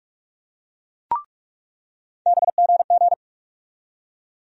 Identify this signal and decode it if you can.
Morse code, heard as a keyed tone, about 46 wpm, 700 Hz. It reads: BGG
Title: USA Call Sign Postfixes - 3 Letters 45wpm